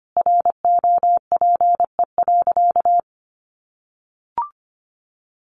Morse code, heard as a keyed tone, about 25 wpm, 700 Hz. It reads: ROPE.
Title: Easy Phrases 25wpm